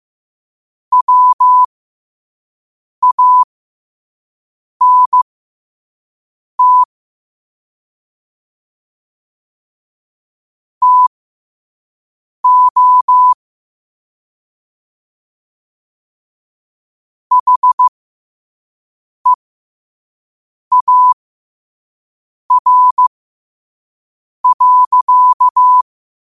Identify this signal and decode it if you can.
Morse code, heard as a keyed tone, about 15 wpm, spaced as for 5 wpm, 1000 Hz. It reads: WANT TO HEAR.